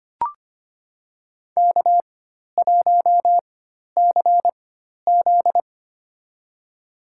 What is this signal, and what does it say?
Morse code, heard as a keyed tone, about 25 wpm, spaced as for 12 wpm, 700 Hz. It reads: K1CZ
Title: USA Call Signs - 4x Character Spacing - 25wpm